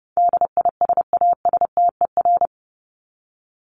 Morse code, heard as a keyed tone, about 30 wpm, 700 Hz. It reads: DISASTER
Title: Top 1501-1750 Words 30wpm